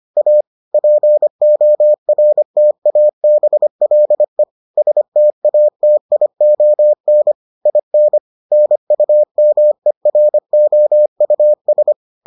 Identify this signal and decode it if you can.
Morse code, heard as a keyed tone, about 25 wpm, 600 Hz. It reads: A PORTABLE STATION IN NUMEROUS